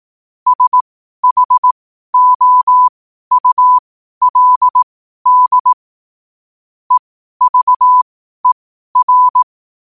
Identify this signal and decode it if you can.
Morse code, heard as a keyed tone, about 18 wpm, spaced as for 12 wpm, 1000 Hz. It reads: SHOULD EVER
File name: qotdfiles / 12wpm